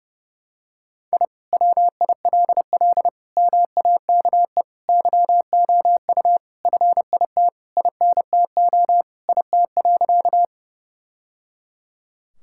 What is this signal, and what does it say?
Morse code, heard as a keyed tone, about 30 wpm, 700 Hz. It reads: I WILL MAKE YOU FIT INTO IT.
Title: Sentences from Top 1000 Words - Review Part 5 - 30wpm